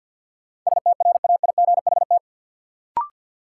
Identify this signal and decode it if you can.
Morse code, heard as a keyed tone, about 52 wpm, 700 Hz. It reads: STRAIGHT
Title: Sentences from Top 1000 Words - New Word Drill - 50wpm